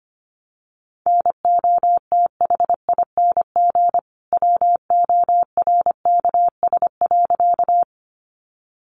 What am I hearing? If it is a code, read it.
Morse code, heard as a keyed tone, about 25 wpm, 700 Hz. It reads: NOTHING WORKS.
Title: Additional Easy Phrases 25wpm